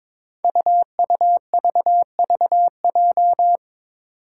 Morse code, heard as a keyed tone, about 22 wpm, 700 Hz. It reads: UUVVJ
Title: Five Challenging Letter-Number Combinations 22wpm